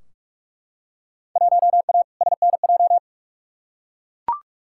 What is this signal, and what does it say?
Morse code, heard as a keyed tone, about 44 wpm, 700 Hz. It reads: 1A SNJ
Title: Field Day Exchanges - 45wpm